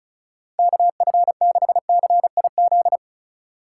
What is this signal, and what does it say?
Morse code, heard as a keyed tone, about 36 wpm, 700 Hz. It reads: KF6CIZ
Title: USA Call Signs - 6 Characters - 2x Character Spacing - 35wpm